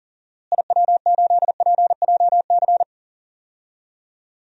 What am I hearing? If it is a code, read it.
Morse code, heard as a keyed tone, about 40 wpm, 700 Hz. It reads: IW8PJC
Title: DXCC-100 Call Signs - 6 Characters - Repeated 2x - 40wpm